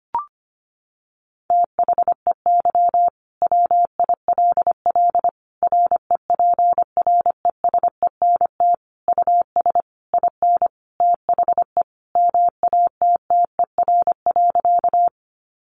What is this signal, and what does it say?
Morse code, heard as a keyed tone, about 25 wpm, 700 Hz. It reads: THEY WILL REPRESENT US IN THE MATTER.
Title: Sentences from Top 1200 Words 25wpm